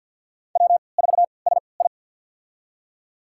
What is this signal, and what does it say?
Morse code, heard as a keyed tone, about 47 wpm, spaced as for 28 wpm, 700 Hz. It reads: W4SI
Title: USA Call Signs - 4 Characters - 3x Character Spacing - 50wpm